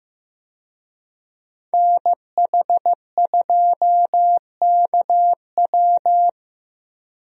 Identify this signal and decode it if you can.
Morse code, heard as a keyed tone, about 15 wpm, 700 Hz. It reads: NH2KW
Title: USA Call Signs - 5 Characters - 15wpm